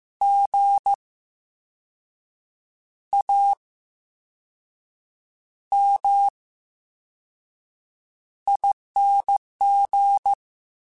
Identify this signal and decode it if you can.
Morse code, heard as a keyed tone, about 15 wpm, spaced as for 5 wpm, 800 Hz. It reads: GAMING
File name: Special INGs-mp3s / Special INGs-mp3s / VerySlowToFaster Ing @13